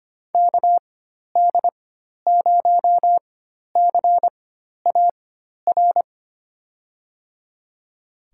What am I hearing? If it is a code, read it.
Morse code, heard as a keyed tone, about 25 wpm, spaced as for 12 wpm, 700 Hz. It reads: KD0CAR
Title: USA Call Signs - 6 Characters - 4x Character Spacing - 25wpm